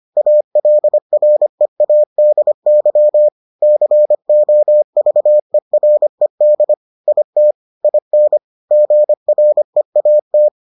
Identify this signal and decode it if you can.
Morse code, heard as a keyed tone, about 25 wpm, 600 Hz. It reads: ALREADY COVERED IT IN GREAT